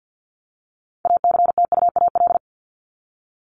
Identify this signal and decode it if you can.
Morse code, heard as a keyed tone, about 48 wpm, 700 Hz. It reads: ACTUAL
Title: Top 751-1000 Words 50wpm